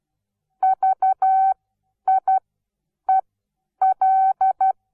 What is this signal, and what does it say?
Morse code, heard as a keyed tone, about 12 wpm, spaced as for 9 wpm, 760 Hz. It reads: VIEL
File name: Onlinelehrgang / morsen / DJ4UF Morsen17